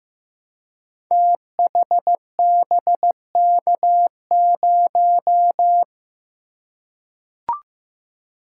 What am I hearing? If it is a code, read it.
Morse code, heard as a keyed tone, about 15 wpm, 700 Hz. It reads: THBK0
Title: Code Groups - 5 Letter:Numbers - 15wpm